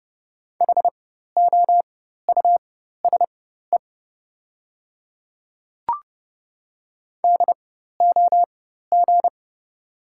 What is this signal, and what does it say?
Morse code, heard as a keyed tone, about 30 wpm, spaced as for 14 wpm, 690 Hz. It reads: HOUSE DOG
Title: 4x Character Spacing - Top 500 Words - 30wpm